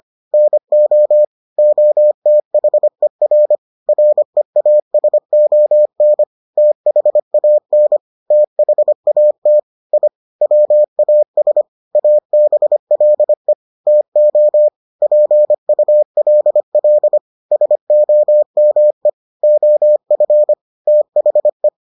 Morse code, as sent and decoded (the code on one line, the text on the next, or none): -. --- / --- - .... .-. / .-. .- ... --- -. / - .... .- -. / - .... .- - / .. / .-- .- ... / .- -... .-.. / - --- / .--. ..- .-.. .-.. / ... --- -- . / --- ..-. / - .... .
NO OTHER REASON THAN THAT I WAS ABLE TO PULL SOME OF THE